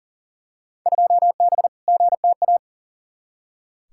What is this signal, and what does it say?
Morse code, heard as a keyed tone, about 40 wpm, 700 Hz. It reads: 2B GTA
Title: Field Day Exchanges Encore - 40wpm